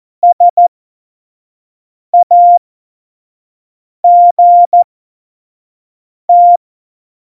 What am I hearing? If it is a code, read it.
Morse code, heard as a keyed tone, about 14 wpm, spaced as for 5 wpm, 700 Hz. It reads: SAGT
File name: Morsekurs / Witze / Witz 13. November 2018, 25 F70